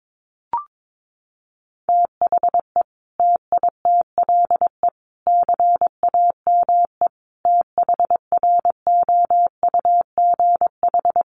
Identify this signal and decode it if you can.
Morse code, heard as a keyed tone, about 22 wpm, 700 Hz. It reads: THE TITLE CAME THROUGH
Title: Sentences from Top 1200 Words 22wpm